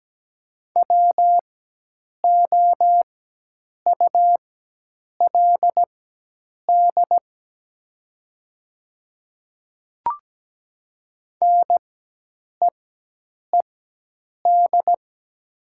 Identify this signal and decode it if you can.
Morse code, heard as a keyed tone, about 17 wpm, spaced as for 8 wpm, 700 Hz. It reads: WOULD NEED
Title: 4x Character Spacing - Top 200 Words - 17wpm